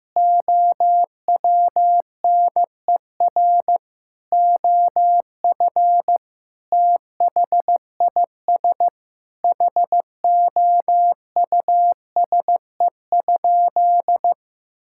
Morse code, heard as a keyed tone, about 15 wpm, 700 Hz. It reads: OWNER OF THIS HOUSE?
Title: Sentences from Top 1600 Words 15wpm